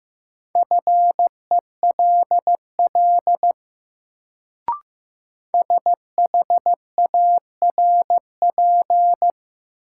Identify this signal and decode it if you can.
Morse code, heard as a keyed tone, about 15 wpm, 700 Hz. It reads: FELL SHARP
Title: Sentences from Top 1600 Words - New Word Drill Alt - 15wpm